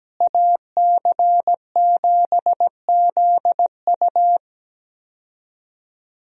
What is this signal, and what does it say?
Morse code, heard as a keyed tone, about 17 wpm, 700 Hz. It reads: AC7ZU
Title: USA Call Signs - 5 Characters - Repeated 3x - 17wpm